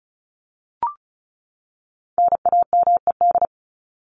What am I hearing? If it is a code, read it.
Morse code, heard as a keyed tone, about 34 wpm, 700 Hz. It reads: NAMED
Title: ED Suffix - Top 100 Words with Base Words 35wpm